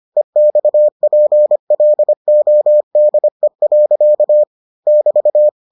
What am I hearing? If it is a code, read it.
Morse code, heard as a keyed tone, about 25 wpm, 600 Hz. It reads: EXPLODE. <BT>